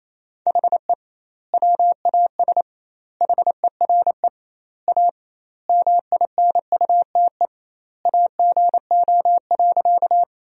Morse code, heard as a keyed tone, about 28 wpm, 700 Hz. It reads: HE WAS HERE A MINUTE AGO.
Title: 2x Word-Spacing - Sentences from Top 500 Words - 28wpm